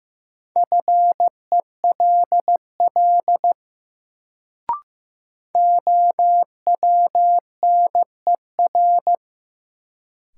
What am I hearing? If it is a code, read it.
Morse code, heard as a keyed tone, about 15 wpm, 700 Hz. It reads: FELL OWNER
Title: Sentences from Top 1600 Words - New Word Drill Alt - 15wpm